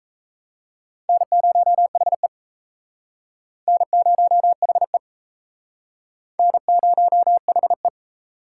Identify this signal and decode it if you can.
Morse code, heard as a keyed tone, about 37 wpm, 700 Hz. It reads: N0HE N0HE N0HE